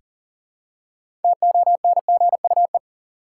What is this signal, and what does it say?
Morse code, heard as a keyed tone, about 40 wpm, 700 Hz. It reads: TONGUE
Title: Six Letter Words - 40wpm